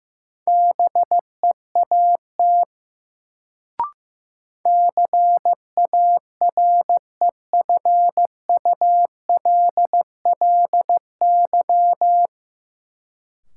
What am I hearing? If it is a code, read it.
Morse code, heard as a keyed tone, about 15 wpm, 700 Hz. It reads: BEAT CAREFULLY
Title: Sentences from Top 1300 Words - New Word Drill Alt - 15wpm